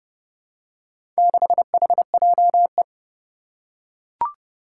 Morse code, heard as a keyed tone, about 30 wpm, 700 Hz. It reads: 6HJE
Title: Four Challenging Letter-Number Combinations 30wpm